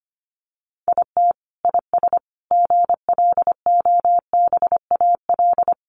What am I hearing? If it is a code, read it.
Morse code, heard as a keyed tone, about 25 wpm, 700 Hz. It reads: IT IS GLOBAL
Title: Sentences from Top 1200 Words 25wpm